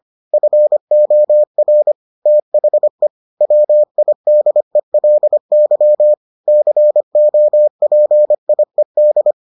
Morse code, heard as a keyed tone, about 25 wpm, 600 Hz. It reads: FOR THE WIDELY COPIED